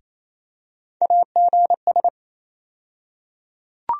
Morse code, heard as a keyed tone, about 28 wpm, 700 Hz. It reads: AGS
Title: USA Call Sign Postfixes - 3 Letters 28wpm